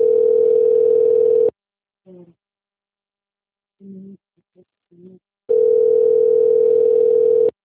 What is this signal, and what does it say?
Ringback tone (North America), 440 + 480 Hz.
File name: rg-1600-unknown- - - .186